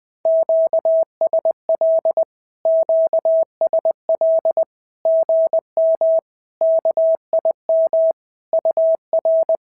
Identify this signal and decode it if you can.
Morse code, heard as a keyed tone, about 20 wpm, 650 Hz. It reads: QSL QSL GM KIM UR
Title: SOTA QSO Practice 20wpm